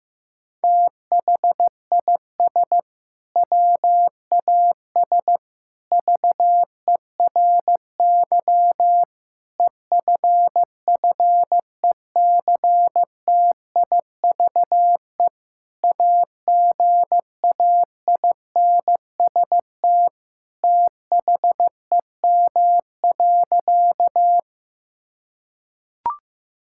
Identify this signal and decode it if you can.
Morse code, heard as a keyed tone, about 15 wpm, 700 Hz. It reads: THIS WAS VERY EFFECTIVE AGAINST THEM.
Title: Sentences from Top 1000 Words - Review Part 6 - 15wpm